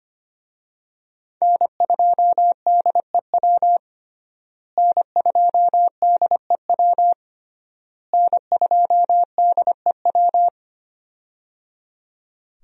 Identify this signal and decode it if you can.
Morse code, heard as a keyed tone, about 25 wpm, 700 Hz. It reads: N2DEW N2DEW N2DEW
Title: USA Call Signs - 5 Characters - Repeated 3x - 25wpm